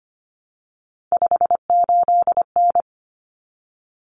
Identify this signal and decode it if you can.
Morse code, heard as a keyed tone, about 25 wpm, 700 Hz. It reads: 58N
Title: Common Cut Numbers - 3-Digit 25wpm